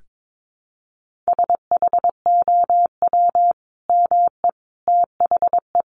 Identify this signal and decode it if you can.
Morse code, heard as a keyed tone, about 22 wpm, 700 Hz. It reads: SHOW ME THE
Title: Sentences from Top 1400 Words 22wpm